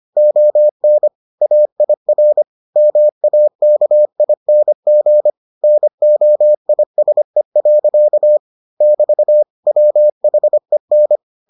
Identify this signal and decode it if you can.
Morse code, heard as a keyed tone, about 25 wpm, 600 Hz. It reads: ON AIR MAKING NOISE. <BT> WHEN